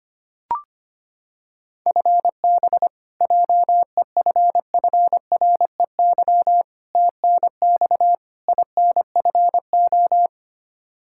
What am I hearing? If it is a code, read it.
Morse code, heard as a keyed tone, about 25 wpm, 700 Hz. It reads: FB JEFFREY TNX INFO
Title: Practice QSO - Exchange 3 - More Info - 25wpm